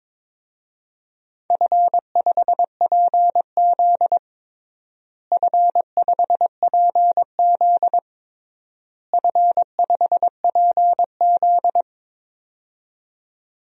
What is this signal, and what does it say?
Morse code, heard as a keyed tone, about 21 wpm, 700 Hz. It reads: F5PZ F5PZ F5PZ
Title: DXCC-100 Call Signs - 4 Characters - Repeated 3x - 22wpm